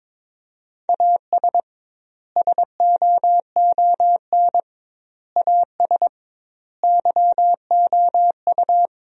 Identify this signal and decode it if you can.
Morse code, heard as a keyed tone, about 22 wpm, 700 Hz. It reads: AS SOON AS YOU
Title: 2x Word-Spacing - Sentences from Top 400 Words - 22wpm